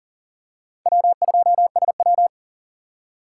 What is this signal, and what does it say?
Morse code, heard as a keyed tone, about 40 wpm, 700 Hz. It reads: W2SW
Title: USA Call Signs - 4 Characters - 3x Character Spacing - 40wpm